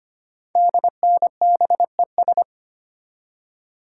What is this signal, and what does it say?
Morse code, heard as a keyed tone, about 25 wpm, 700 Hz. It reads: DNBES
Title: Code Groups - 5 Letters - 25wpm